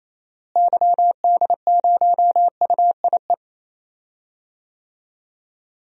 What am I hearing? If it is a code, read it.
Morse code, heard as a keyed tone, about 28 wpm, 700 Hz. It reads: YD0UIE
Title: DXCC-100 Call Signs - 6 Characters - Repeated 2x - 28wpm